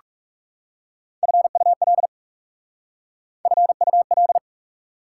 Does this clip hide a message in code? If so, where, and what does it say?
Yes: Morse code from 1.2 to 4.4 s, heard as a keyed tone, about 40 wpm, 690 Hz. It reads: FUL FUL